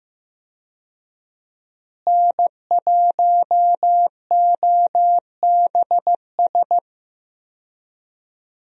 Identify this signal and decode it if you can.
Morse code, heard as a keyed tone, about 16 wpm, 700 Hz. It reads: N1OBS
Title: USA Call Signs - 5 Characters - Repeated 3x - 15wpm